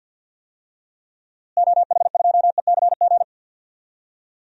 Morse code, heard as a keyed tone, about 50 wpm, 700 Hz. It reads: OH2ECG
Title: DXCC-100 Call Signs - 6 Characters - Repeated 3x - 50wpm